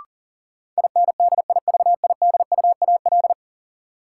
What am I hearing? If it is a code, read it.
Morse code, heard as a keyed tone, about 40 wpm, 700 Hz. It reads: INDIVIDUAL
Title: Sentences from Top 1000 Words - Review Part 4 - New Word Drill - 40wpm